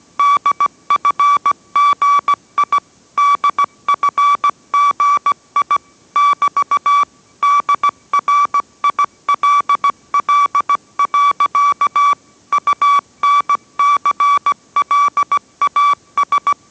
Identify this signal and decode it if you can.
Morse code, heard as a keyed tone, about 17 wpm, 1150 Hz. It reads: DFGI DFGI <BT> DRILL. UNCLAS